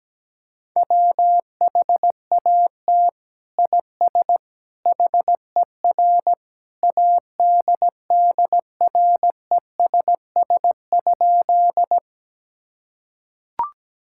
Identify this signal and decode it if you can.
Morse code, heard as a keyed tone, about 17 wpm, 700 Hz. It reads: WHAT IS HER ADDRESS?
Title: Sentences from Top 1000 Words - Review Part 6 - 17wpm